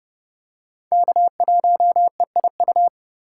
Morse code, heard as a keyed tone, about 30 wpm, 700 Hz. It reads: K1EIU
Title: USA Call Signs - 5 Characters - 4x Character Spacing - 30wpm